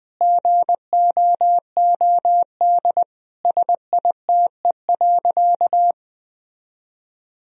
Morse code, heard as a keyed tone, about 20 wpm, 700 Hz. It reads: GOOD SITE.
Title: Sentences from Top 1000 Words - Review Part 4 - 20wpm